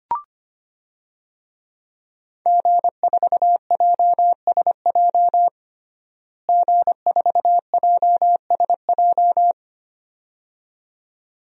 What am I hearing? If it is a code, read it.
Morse code, heard as a keyed tone, about 25 wpm, 700 Hz. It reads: G4JSJ G4JSJ
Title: DXCC-100 Call Signs - 5 Characters - Repeated 2x - 25wpm